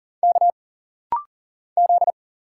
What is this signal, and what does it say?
Morse code, heard as a keyed tone, about 40 wpm, 700 Hz. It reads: K Z
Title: Single Letters - Rapid-Fire 40wpm